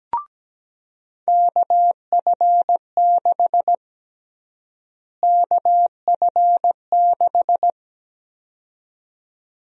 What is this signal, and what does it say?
Morse code, heard as a keyed tone, about 17 wpm, 700 Hz. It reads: KF6 KF6
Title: USA Call Sign Prefixes - Repeated 2x - 17wpm